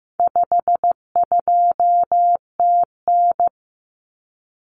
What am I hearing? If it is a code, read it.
Morse code, heard as a keyed tone, about 15 wpm, 700 Hz. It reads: 52TN
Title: Common Cut Numbers - 4-Digit 15wpm